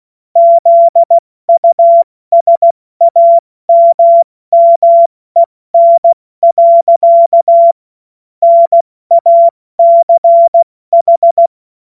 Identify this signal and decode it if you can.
Morse code, heard as a keyed tone, about 16 wpm, 680 Hz. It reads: ZUSAMMEN. NACH
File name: Morsekurs / Witze / Witz 21. Dezember 2021, 70 F80